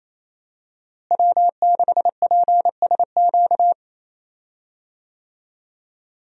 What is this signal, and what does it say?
Morse code, heard as a keyed tone, about 28 wpm, 700 Hz. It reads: W6PSQ
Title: USA Call Signs - 5 Characters - Repeated 3x - 28wpm